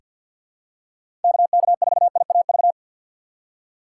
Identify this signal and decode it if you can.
Morse code, heard as a keyed tone, about 47 wpm, 700 Hz. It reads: KK4IAV
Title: USA Call Signs - 6 Characters - 3x Character Spacing - 50wpm